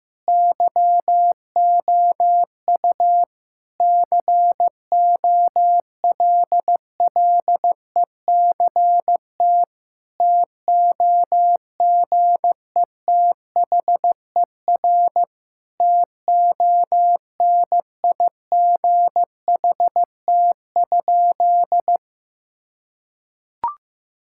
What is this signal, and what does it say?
Morse code, heard as a keyed tone, about 15 wpm, 700 Hz. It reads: YOU COLLECT TOGETHER TONIGHT?
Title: Sentences from Top 1600 Words 15wpm